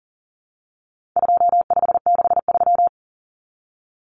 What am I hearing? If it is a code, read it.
Morse code, heard as a keyed tone, about 40 wpm, 700 Hz. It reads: 2563
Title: Common Cut Numbers - 4-Digit 40wpm